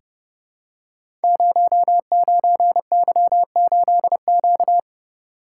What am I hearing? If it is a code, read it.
Morse code, heard as a keyed tone, about 30 wpm, 700 Hz. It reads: 09Y8Q
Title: Code Groups - 5 Letter:Numbers - 30wpm